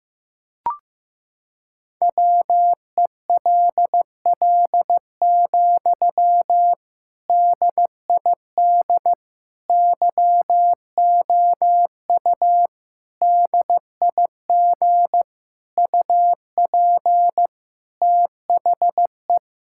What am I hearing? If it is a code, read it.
Morse code, heard as a keyed tone, about 16 wpm, 700 Hz. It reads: WELL, DID YOU DIG UP THE
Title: Sentences from Top 1600 Words 15wpm